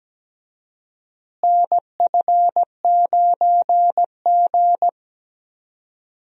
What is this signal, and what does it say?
Morse code, heard as a keyed tone, about 17 wpm, 700 Hz. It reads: NF9G
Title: USA Call Signs - 4 Characters - Repeated 3x - 17wpm